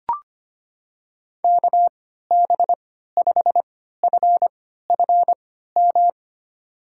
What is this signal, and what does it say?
Morse code, heard as a keyed tone, about 25 wpm, spaced as for 14 wpm, 700 Hz. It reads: KB5FFM
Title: USA Call Signs - 6 Characters - 3x Character Spacing - 25wpm